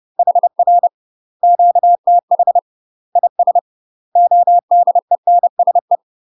Morse code, heard as a keyed tone, about 30 wpm, 700 Hz. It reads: HR QTH IS ODENSE
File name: UK MASTER QSOs TRAINING / 30WPM / UK MASTER QSOs 0007